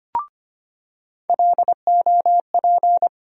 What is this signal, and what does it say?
Morse code, heard as a keyed tone, about 25 wpm, 700 Hz. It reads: LOP
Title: USA Call Sign Postfixes - 3 Letters 25wpm